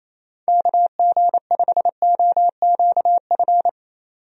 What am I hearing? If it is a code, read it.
Morse code, heard as a keyed tone, about 28 wpm, 700 Hz. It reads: KG5OQF
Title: USA Call Signs - 6 Characters - 28wpm